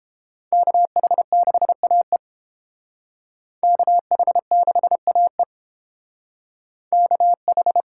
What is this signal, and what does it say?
Morse code, heard as a keyed tone, about 29 wpm, 700 Hz. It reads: KH6AE KH6AE KH